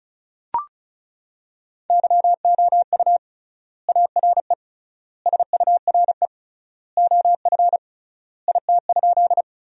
Morse code, heard as a keyed tone, about 36 wpm, 700 Hz. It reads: YOU ARE SURE OF IT?
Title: 3x Word-Spacing - Sentences from Top 300 Words - 35wpm